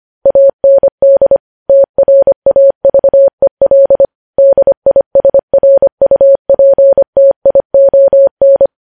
Morse code, heard as a keyed tone, about 25 wpm, 550 Hz. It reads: AND TRAVEL DISRUPTION
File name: podcast / News headlines-1x-25x25